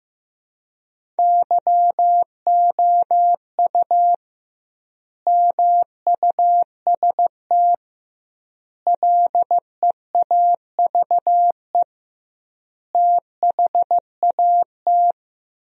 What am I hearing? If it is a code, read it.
Morse code, heard as a keyed tone, about 15 wpm, 700 Hz. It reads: YOU MUST LEAVE THAT
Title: 2x Word-Spacing - Sentences from Top 400 Words - 15wpm